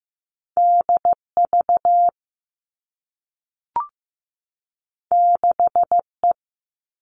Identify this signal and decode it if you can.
Morse code, heard as a keyed tone, about 15 wpm, 700 Hz. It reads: DV 6E
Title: All Cut Numbers - 2-Digit 15wpm